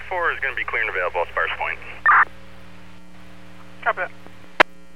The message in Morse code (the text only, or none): none